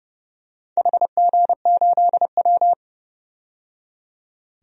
Morse code, heard as a keyed tone, about 31 wpm, 700 Hz. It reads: HG8W
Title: DXCC-100 Call Signs - 4 Characters - 30wpm